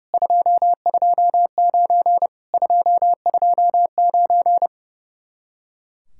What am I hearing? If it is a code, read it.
Morse code, heard as a keyed tone, about 37 wpm, 700 Hz. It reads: 229 229
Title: Speed-Racing Common QSO Phrases - 30wpm